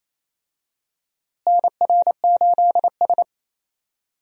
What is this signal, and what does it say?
Morse code, heard as a keyed tone, about 28 wpm, 700 Hz. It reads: NR8S
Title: USA Call Signs - 4 Characters - Repeated 2x - 28wpm